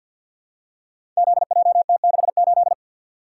Morse code, heard as a keyed tone, about 47 wpm, 700 Hz. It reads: ZJT68
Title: Code Groups - 5 Letter:Numbers - 50wpm